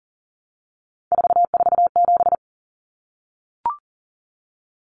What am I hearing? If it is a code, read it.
Morse code, heard as a keyed tone, about 38 wpm, 700 Hz. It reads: 447
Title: Common Cut Numbers - 3-Digit 40wpm